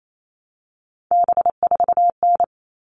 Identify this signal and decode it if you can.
Morse code, heard as a keyed tone, about 28 wpm, 700 Hz. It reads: B4N
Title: All Cut Numbers - 3-Digit 28wpm